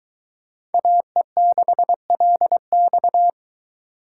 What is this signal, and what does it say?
Morse code, heard as a keyed tone, about 22 wpm, 700 Hz. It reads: AE6LX